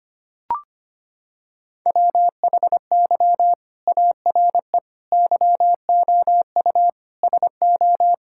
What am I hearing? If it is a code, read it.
Morse code, heard as a keyed tone, about 25 wpm, 700 Hz. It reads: WHY ARE YOU SO
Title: Sentences from Top 1000 Words - Review Part 6 - 25wpm